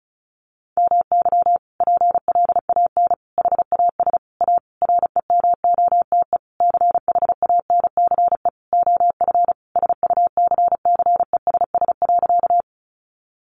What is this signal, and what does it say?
Morse code, heard as a keyed tone, about 34 wpm, 700 Hz. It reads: MY PLAN HAS A REMOTE CHANCE OF SUCCESS.
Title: Sentences from Top 1300 Words 35wpm